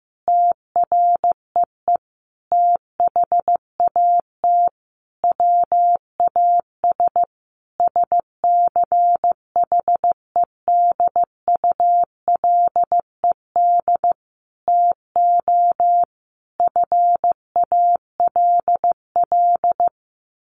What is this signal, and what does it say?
Morse code, heard as a keyed tone, about 15 wpm, 700 Hz. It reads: TREE THAT WAS SCHEDULED TO FALL
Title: A Sound Of Thunder 15wpm